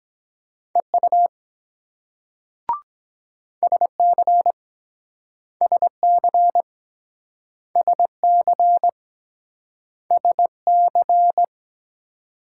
Morse code, heard as a keyed tone, about 23 wpm, 700 Hz. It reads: EU SC SC SC SC